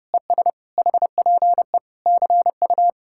Morse code, heard as a keyed tone, about 37 wpm, 700 Hz. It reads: ES HPE CU
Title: Speed-Racing Common QSO Phrases - 30wpm